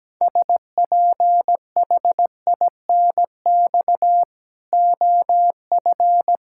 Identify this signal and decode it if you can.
Morse code, heard as a keyed tone, about 17 wpm, 700 Hz. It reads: SPHINX OF
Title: Pangram Challenge v2 - 17wpm